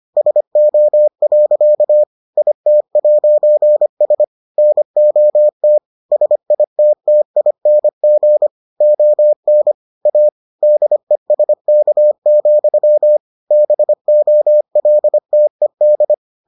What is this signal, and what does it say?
Morse code, heard as a keyed tone, about 25 wpm, 600 Hz. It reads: SO. IT'S NOT SITTING ON A DESK, BOLTED